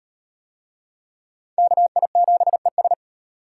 Morse code, heard as a keyed tone, about 34 wpm, 700 Hz. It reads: KI7ES